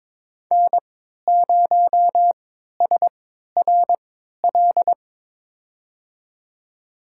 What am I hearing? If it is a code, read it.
Morse code, heard as a keyed tone, about 22 wpm, spaced as for 13 wpm, 700 Hz. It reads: N0SRL
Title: USA Call Signs - 5 Characters - 3x Character Spacing - 22wpm